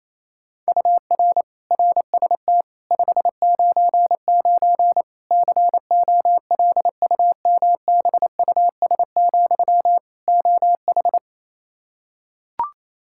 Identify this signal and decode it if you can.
Morse code, heard as a keyed tone, about 27 wpm, 700 Hz. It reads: UR RST 599 COLUMBUS, OH
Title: K1USN SST QSO Practice 28wpm